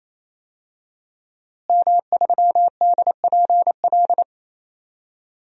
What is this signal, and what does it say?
Morse code, heard as a keyed tone, about 28 wpm, 700 Hz. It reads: M3DPL
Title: DXCC-100 Call Signs - 5 Characters - Repeated 3x - 28wpm